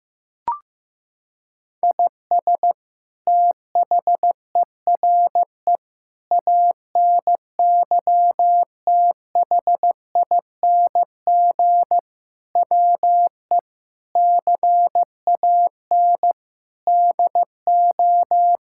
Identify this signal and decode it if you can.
Morse code, heard as a keyed tone, about 15 wpm, 700 Hz. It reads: IS THERE ANYTHING WE CAN DO
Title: Sentences from Top 1000 Words - Review Part 6 - 15wpm